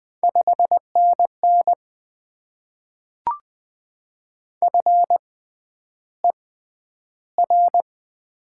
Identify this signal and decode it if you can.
Morse code, heard as a keyed tone, about 20 wpm, spaced as for 10 wpm, 700 Hz. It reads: 5NN FER
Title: Top 20 QSO Elements - 6x Character Spacing - 20wpm